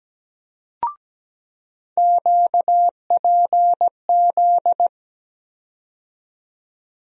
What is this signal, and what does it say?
Morse code, heard as a keyed tone, about 17 wpm, 700 Hz. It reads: QPZ
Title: USA Call Sign Postfixes - 3 Letters 17wpm